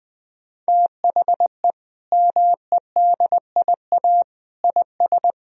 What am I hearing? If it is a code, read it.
Morse code, heard as a keyed tone, about 20 wpm, 700 Hz. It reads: THE MEDIA IS
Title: Sentences from Top 1000 Words - Review Part 4 - 20wpm